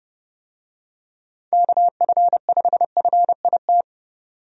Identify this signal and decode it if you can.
Morse code, heard as a keyed tone, about 29 wpm, 700 Hz. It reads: KF5FIT
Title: USA Call Signs - 6 Characters - 30wpm